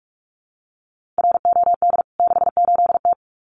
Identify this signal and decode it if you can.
Morse code, heard as a keyed tone, about 44 wpm, 700 Hz. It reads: ROD 68T
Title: CWT Exchange 45wpm